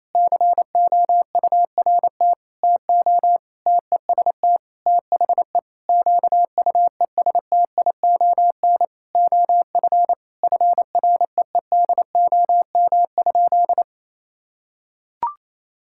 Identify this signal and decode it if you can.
Morse code, heard as a keyed tone, about 28 wpm, 700 Hz. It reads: COURT TO TEST THE QUESTION OF FREEDOM?
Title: Sentences from Top 1600 Words 28wpm